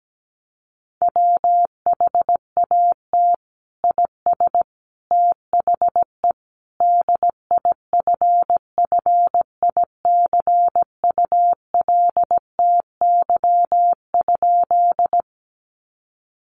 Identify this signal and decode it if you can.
Morse code, heard as a keyed tone, about 17 wpm, 700 Hz. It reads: WHAT IS THE DIFFICULTY?
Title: Sentences from Top 1500 Words 17wpm